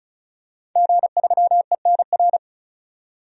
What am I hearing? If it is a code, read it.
Morse code, heard as a keyed tone, about 36 wpm, 700 Hz. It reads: G3ENR
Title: DXCC-100 Call Signs - 5 Characters - 35wpm